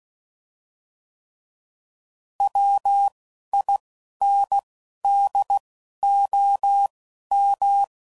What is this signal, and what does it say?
Morse code, heard as a keyed tone, about 16 wpm, spaced as for 12 wpm, 800 Hz. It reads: WINDOM
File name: CW Code Groups @10WPM / fromSM's / Antennas